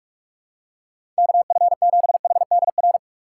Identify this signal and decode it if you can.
Morse code, heard as a keyed tone, about 42 wpm, 700 Hz. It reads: KF7HDR
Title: USA Call Signs - 6 Characters - 4x Character Spacing - 45wpm